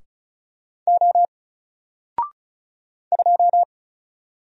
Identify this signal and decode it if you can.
Morse code, heard as a keyed tone, about 36 wpm, 700 Hz. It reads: O 2
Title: Single Letter-Number 35wpm